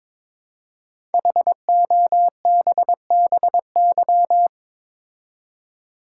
Morse code, heard as a keyed tone, about 22 wpm, 700 Hz. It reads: HOBBY
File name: 5 Letter Words - First 3 Letters Given - 22wpm